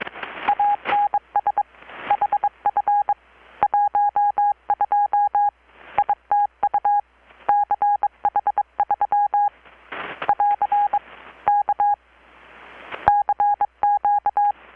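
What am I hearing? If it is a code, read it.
Morse code, heard as a keyed tone, about 22 wpm, 820 Hz. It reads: ANS HF 12 ITU CH3 <AR> K CQ
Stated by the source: Morse from KPH's Night of Nights XX maritime radiotelegraphy event, received in Chiapas, Mexico about 3,800 km away on a Kenwood R-600 with a grounded 10 m long-wire antenna.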